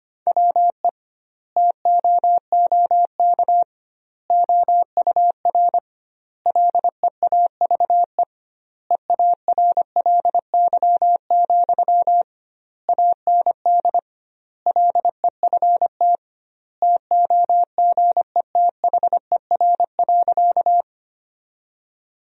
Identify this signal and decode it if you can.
Morse code, heard as a keyed tone, about 25 wpm, 700 Hz. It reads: WE TOOK OUR LEAVE EARLY, AND LEFT TOGETHER.
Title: 2x Word-Spacing - Sentences from Top 400 Words - 25wpm